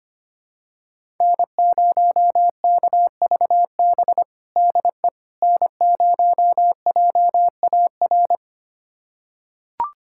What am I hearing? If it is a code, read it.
Morse code, heard as a keyed tone, about 25 wpm, 700 Hz. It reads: N0KVB DE N0JAR
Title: USA Callsign DE Callsign - 5 Characters - 25wpm